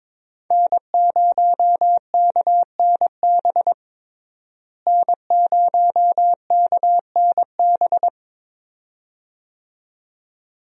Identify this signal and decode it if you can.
Morse code, heard as a keyed tone, about 22 wpm, 700 Hz. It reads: N0KNB N0KNB
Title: USA Call Signs - 5 Characters - Repeated 2x - 22wpm